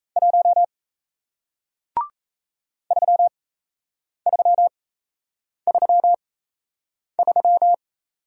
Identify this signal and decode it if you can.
Morse code, heard as a keyed tone, about 37 wpm, spaced as for 8 wpm, 700 Hz. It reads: 1 3333